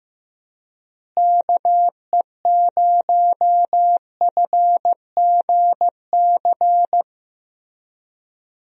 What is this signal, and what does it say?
Morse code, heard as a keyed tone, about 16 wpm, 700 Hz. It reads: KE0FGC
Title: USA Call Signs - 6 Characters - Repeated 3x - 15wpm